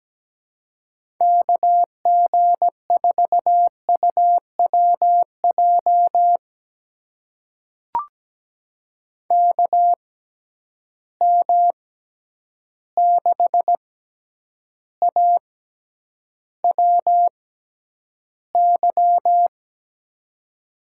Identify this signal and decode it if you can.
Morse code, heard as a keyed tone, about 17 wpm, spaced as for 9 wpm, 700 Hz. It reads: KG4UWJ KM6AWY
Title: USA Call Signs - 6 Characters - 6x Character Spacing - 17wpm